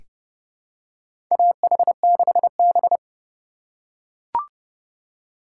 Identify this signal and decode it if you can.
Morse code, heard as a keyed tone, about 30 wpm, 700 Hz. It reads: AH6B
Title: USA Call Signs - 4 Characters - 2x Character Spacing - 30wpm